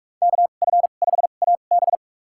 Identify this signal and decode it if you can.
Morse code, heard as a keyed tone, about 42 wpm, spaced as for 31 wpm, 700 Hz. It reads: KF5AB
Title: USA Call Signs - 2x Character Spacing - 45wpm